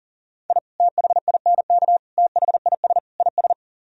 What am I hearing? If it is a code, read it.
Morse code, heard as a keyed tone, about 48 wpm, 700 Hz. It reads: I THINK THIS IS
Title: Speed-Racing - Sets of 4 Words 40wpm